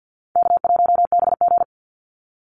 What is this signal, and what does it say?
Morse code, heard as a keyed tone, about 50 wpm, 700 Hz. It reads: K1BG
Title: Call Signs - 50wpm